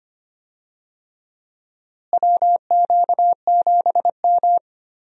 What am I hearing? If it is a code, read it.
Morse code, heard as a keyed tone, about 25 wpm, 700 Hz. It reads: WQ7M
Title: USA Call Signs - 4 Characters - Repeated 3x - 25wpm